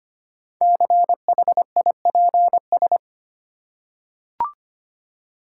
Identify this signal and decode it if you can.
Morse code, heard as a keyed tone, about 25 wpm, 700 Hz. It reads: CHIPS
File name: practice / 25wpm / Easy Words with Hints - 25wpm